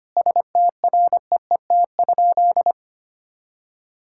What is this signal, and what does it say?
Morse code, heard as a keyed tone, about 25 wpm, 700 Hz. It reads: STREET?
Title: Sentences from Top 1000 Words - Review Part 6 - 25wpm